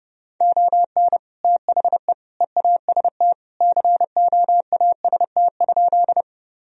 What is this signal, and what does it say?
Morse code, heard as a keyed tone, about 32 wpm, 700 Hz. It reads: ON THE EAST COAST?
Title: Sentences from Top 1000 Words - Review Part 4 - 30wpm